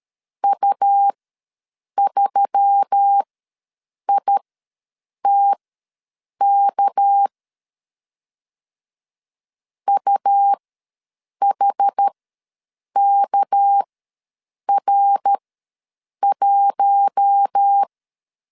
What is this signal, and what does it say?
Morse code, heard as a keyed tone, about 13 wpm, spaced as for 7 wpm, 800 Hz. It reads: U3ITK UHKR1
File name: LEZIONI / 6 PKX0 / 6 PKX ws 12wpm 3106